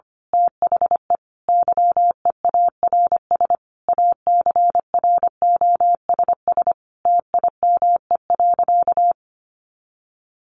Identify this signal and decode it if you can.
Morse code, heard as a keyed tone, about 25 wpm, 700 Hz. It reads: THE YEARS ACROSS TIME.
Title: A Sound Of Thunder 25wpm